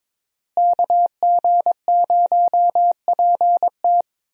Morse code, heard as a keyed tone, about 22 wpm, 700 Hz. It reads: KG0PT
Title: USA Call Signs - 5 Characters - Repeated 3x - 22wpm